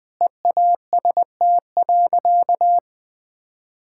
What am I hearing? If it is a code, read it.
Morse code, heard as a keyed tone, about 21 wpm, 700 Hz. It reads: EAST.
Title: Sentences with 3-Word Phrases - 20wpm